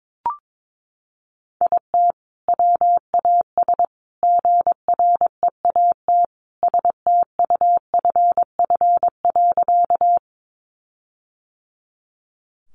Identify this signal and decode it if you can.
Morse code, heard as a keyed tone, about 22 wpm, 700 Hz. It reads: IT WAS GREAT STUFF.
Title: Sentences from Top 1400 Words 22wpm